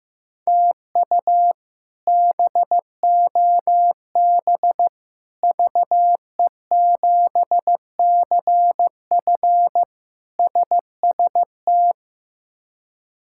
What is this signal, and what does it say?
Morse code, heard as a keyed tone, about 15 wpm, 700 Hz. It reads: TU BOB VE7CF SST
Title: K1USN SST QSO Practice - 15wpm